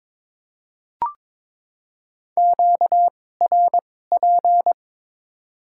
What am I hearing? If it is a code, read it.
Morse code, heard as a keyed tone, about 22 wpm, spaced as for 16 wpm, 700 Hz. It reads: QRP
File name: Top 90 QSO Elements - 2x Character Spacing - 22wpm